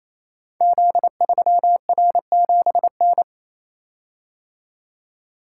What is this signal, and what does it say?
Morse code, heard as a keyed tone, about 28 wpm, 700 Hz. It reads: Z3R7N
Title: Code Groups - 5 Letter:Numbers - 28wpm